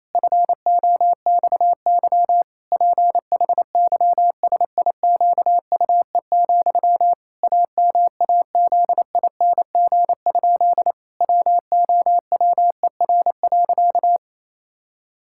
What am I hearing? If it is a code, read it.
Morse code, heard as a keyed tone, about 28 wpm, 700 Hz. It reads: FOXY PHYSIQUE, AMAZING? WOWER.
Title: Pangram Challenge v2 - 28wpm